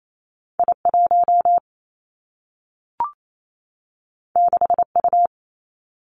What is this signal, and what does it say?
Morse code, heard as a keyed tone, about 28 wpm, 700 Hz. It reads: I 1 6 U
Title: Two Challenging Letter-Number Combinations 28wpm